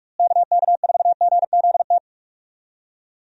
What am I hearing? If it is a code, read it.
Morse code, heard as a keyed tone, about 43 wpm, 700 Hz. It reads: KK4GZT
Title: USA Call Signs - 6 Characters - Repeated 3x - 45wpm